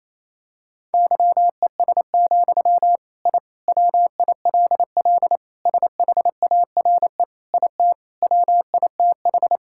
Morse code, heard as a keyed tone, about 28 wpm, 700 Hz. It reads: YES, I WILL SHARE IT WITH
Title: Sentences from Top 1000 Words - Review Part 6 - 28wpm